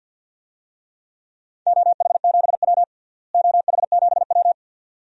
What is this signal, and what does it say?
Morse code, heard as a keyed tone, about 49 wpm, 700 Hz. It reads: OH7W OH7W
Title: DXCC-100 Call Signs - 4 Characters - Repeated 2x - 50wpm